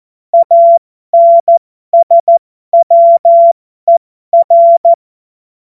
Morse code, heard as a keyed tone, about 14 wpm, spaced as for 12 wpm, 680 Hz. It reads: ANSWER
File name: Morsekurs / Witze